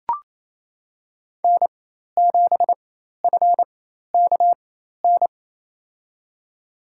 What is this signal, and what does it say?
Morse code, heard as a keyed tone, about 28 wpm, spaced as for 13 wpm, 700 Hz. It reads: N7FKN
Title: USA Call Signs - 5 Characters - 4x Character Spacing - 28wpm